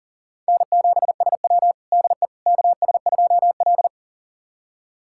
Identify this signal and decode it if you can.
Morse code, heard as a keyed tone, about 40 wpm, 700 Hz. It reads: N7SW DE KS2L
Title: USA Callsign DE Callsign - 4 Characters - 40wpm